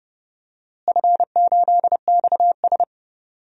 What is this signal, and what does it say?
Morse code, heard as a keyed tone, about 29 wpm, 700 Hz. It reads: F8XS
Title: DXCC-100 Call Signs - 4 Characters - Repeated 3x - 30wpm